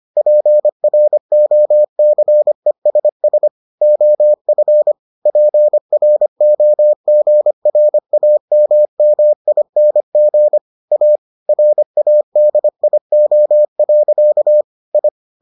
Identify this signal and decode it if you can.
Morse code, heard as a keyed tone, about 25 wpm, 600 Hz. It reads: PROCESS OF PROGRAMMING A RADIO. I